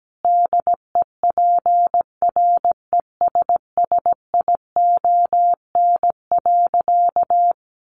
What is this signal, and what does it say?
Morse code, heard as a keyed tone, about 17 wpm, 700 Hz. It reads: DEPRESSION.
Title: Sentences from Top 1300 Words 17wpm